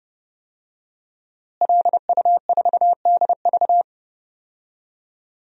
Morse code, heard as a keyed tone, about 30 wpm, 700 Hz. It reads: LU4DV
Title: DXCC-100 Call Signs - 5 Characters - Repeated 2x - 30wpm